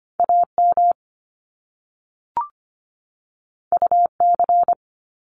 Morse code, heard as a keyed tone, about 25 wpm, 700 Hz. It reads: AM UC
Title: Top Two Letter Combinations in Words 25wpm